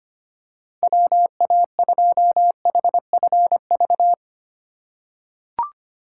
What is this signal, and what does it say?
Morse code, heard as a keyed tone, about 25 wpm, 700 Hz. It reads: WA2HFV
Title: USA Call Signs - 6 Characters - 4x Character Spacing - 25wpm